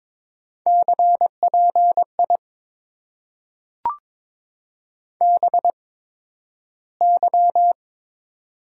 Morse code, heard as a keyed tone, about 22 wpm, spaced as for 12 wpm, 700 Hz. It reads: CPI BY
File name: Top 90 QSO Elements - 8x Character Spacing - 22wpm